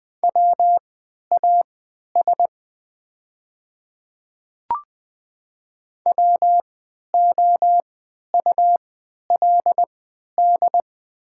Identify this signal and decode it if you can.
Morse code, heard as a keyed tone, about 20 wpm, spaced as for 11 wpm, 700 Hz. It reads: WAS WOULD
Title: 3x Character Spacing - Top 100 Words - 20wpm